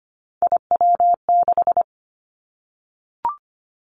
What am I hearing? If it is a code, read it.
Morse code, heard as a keyed tone, about 25 wpm, 700 Hz. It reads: IW6
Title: Three Challenging Letter-Number Combinations 25wpm